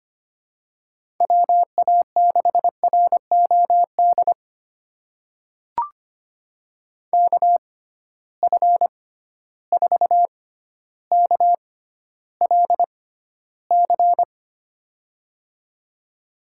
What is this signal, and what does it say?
Morse code, heard as a keyed tone, about 25 wpm, spaced as for 13 wpm, 700 Hz. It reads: WA6ROD KF4KLC
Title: USA Call Signs - 6 Characters - 6x Character Spacing - 25wpm